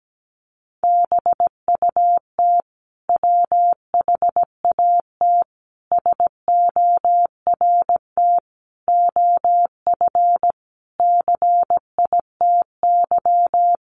Morse code, heard as a keyed tone, about 17 wpm, 700 Hz. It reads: BUT WHAT SORT OF CITY
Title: Sentences from Top 1300 Words 17wpm